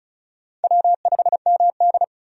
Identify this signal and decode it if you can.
Morse code, heard as a keyed tone, about 36 wpm, 700 Hz. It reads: W5MD
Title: USA Call Signs - 4 Characters - 35wpm